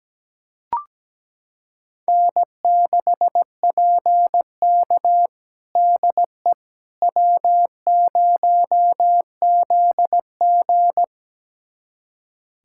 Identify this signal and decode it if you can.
Morse code, heard as a keyed tone, about 17 wpm, 700 Hz. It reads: N6PK DE W0ZG
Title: USA Callsign DE Callsign - 4 Characters - 17wpm